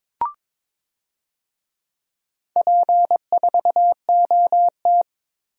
Morse code, heard as a keyed tone, about 22 wpm, 700 Hz. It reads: P4OT
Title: DXCC-100 Call Signs - 4 Characters - Repeated 2x - 22wpm